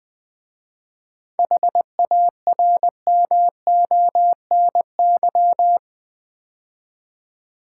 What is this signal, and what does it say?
Morse code, heard as a keyed tone, about 20 wpm, 700 Hz. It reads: HARMONY
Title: 7 Letter Words - First 3 Letters Given - 20wpm